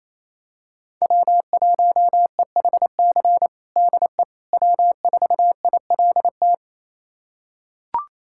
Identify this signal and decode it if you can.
Morse code, heard as a keyed tone, about 28 wpm, 700 Hz. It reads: W1EHC DE W4ILT
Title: USA Callsign DE Callsign - 5 Characters - 28wpm